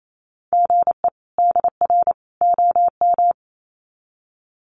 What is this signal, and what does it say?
Morse code, heard as a keyed tone, about 28 wpm, 700 Hz. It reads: GE DR OM
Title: Common QSO Phrases 28wpm